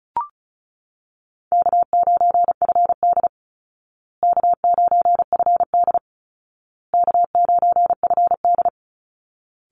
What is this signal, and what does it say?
Morse code, heard as a keyed tone, about 34 wpm, 700 Hz. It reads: K9FD K9FD K9FD
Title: Call Signs Repeated 2x - 35wpm